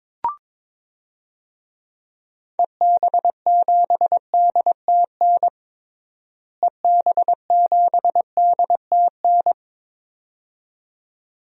Morse code, heard as a keyed tone, about 21 wpm, 700 Hz. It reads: EB7DTN EB7DTN
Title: DXCC-100 Call Signs - 6 Characters - Repeated 2x - 22wpm